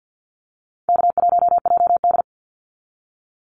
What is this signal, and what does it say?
Morse code, heard as a keyed tone, about 48 wpm, 700 Hz. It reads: K1JD
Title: NA SOTA Activator Callsigns 50wpm